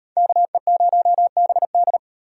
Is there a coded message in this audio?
Yes: Morse code from 0.2 to 2.0 s, heard as a keyed tone, about 38 wpm, 700 Hz. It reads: KE0BD